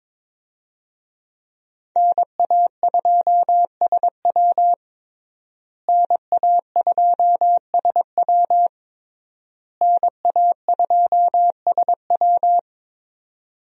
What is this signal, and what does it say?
Morse code, heard as a keyed tone, about 22 wpm, 700 Hz. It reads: NA2SW NA2SW NA2SW
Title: USA Call Signs - 5 Characters - Repeated 3x - 22wpm